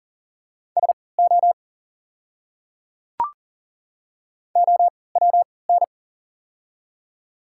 Morse code, heard as a keyed tone, about 40 wpm, spaced as for 23 wpm, 700 Hz. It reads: SO OWN
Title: 3x Character Spacing - Top 300 Words - 40wpm